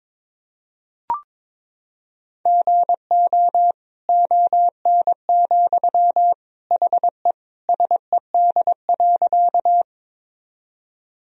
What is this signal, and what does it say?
Morse code, heard as a keyed tone, about 22 wpm, 700 Hz. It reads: GO ON, HE SED.
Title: Sentences from Top 100 Words and Abbreviations 22wpm